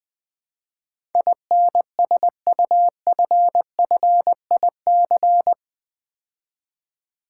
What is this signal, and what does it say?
Morse code, heard as a keyed tone, about 20 wpm, 700 Hz. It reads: INSUFFIC
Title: Build A Word - 4 Letters At A Time - 12-Letter Words - 20wpm